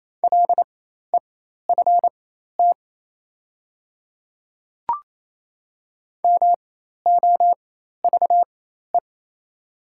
Morse code, heard as a keyed tone, about 27 wpm, spaced as for 13 wpm, 700 Hz. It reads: LEFT MOVE
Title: 4x Character Spacing - Top 500 Words - 28wpm